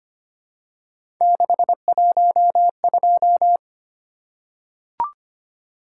Morse code, heard as a keyed tone, about 25 wpm, 700 Hz. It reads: 612
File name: Three Letter-Number Combinations - 25wpm